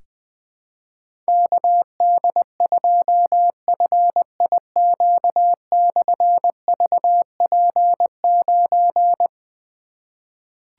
Morse code, heard as a keyed tone, about 20 wpm, 700 Hz. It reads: KD2FIQ/VP9
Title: Difficult Call Signs 20wpm